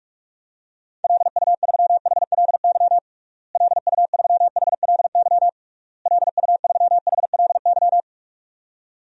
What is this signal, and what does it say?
Morse code, heard as a keyed tone, about 41 wpm, 700 Hz. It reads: LU3HLY LU3HLY LU3HLY
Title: DXCC-100 Call Signs - 6 Characters - Repeated 3x - 45wpm